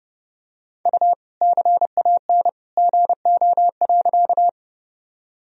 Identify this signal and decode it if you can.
Morse code, heard as a keyed tone, about 32 wpm, 700 Hz. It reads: U CAN GO.
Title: Sentences from Top 100 Words and Abbreviations 30wpm